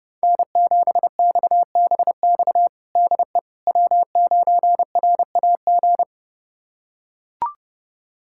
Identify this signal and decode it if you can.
Morse code, heard as a keyed tone, about 30 wpm, 700 Hz. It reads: N7XBX DE W9RAG
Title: USA Callsign DE Callsign - 5 Characters - 30wpm